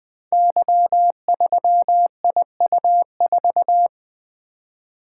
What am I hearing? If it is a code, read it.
Morse code, heard as a keyed tone, about 20 wpm, 700 Hz. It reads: Y3IU4
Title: Code Groups - 5 Letter:Numbers - 20wpm